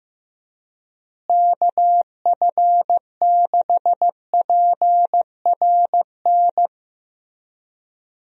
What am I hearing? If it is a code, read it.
Morse code, heard as a keyed tone, about 15 wpm, 700 Hz. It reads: KF6PRN
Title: USA Call Signs - 6 Characters - Repeated 2x - 15wpm